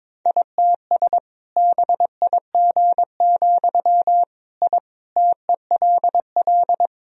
Morse code, heard as a keyed tone, about 22 wpm, 700 Hz. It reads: ITS BIG, I TELL
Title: Sentences from Top 1000 Words - Review Part 1 - 22wpm